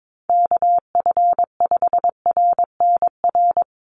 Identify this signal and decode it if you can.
Morse code, heard as a keyed tone, about 22 wpm, 700 Hz. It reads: KF5RNR
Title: USA Call Signs 22wpm